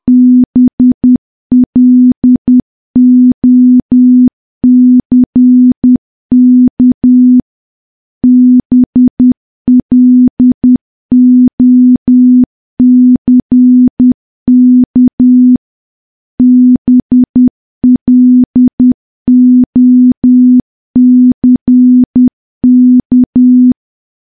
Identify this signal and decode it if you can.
Morse code, heard as a keyed tone, about 10 wpm, 250 Hz. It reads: BLOCK BLOCK BLOCK
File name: morse / morse-training-10WPM